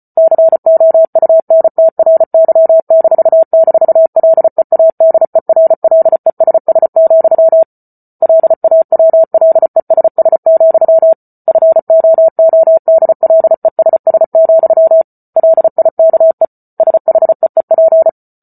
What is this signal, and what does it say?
Morse code, heard as a keyed tone, about 34 wpm, 650 Hz. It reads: COUNTRY--LEADERLESS, LAWLESS, FOODLESS, LIKE SHEEP